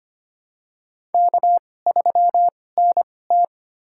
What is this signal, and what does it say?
Morse code, heard as a keyed tone, about 25 wpm, spaced as for 18 wpm, 700 Hz. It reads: K3NT
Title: USA Call Signs - 4 Characters - 2x Character Spacing - 25wpm